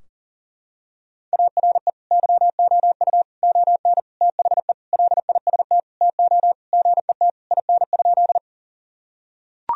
Morse code, heard as a keyed tone, about 40 wpm, 700 Hz. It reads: ARE YOU ON THE LIST TO GET IN?
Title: Sentences from Top 1000 Words - Review Part 2 - 40wpm